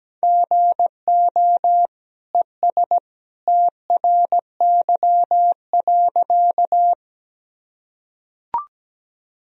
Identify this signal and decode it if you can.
Morse code, heard as a keyed tone, about 17 wpm, 700 Hz. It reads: GO ES TRY.
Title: Sentences from Top 200 Words and Abbreviations 17wpm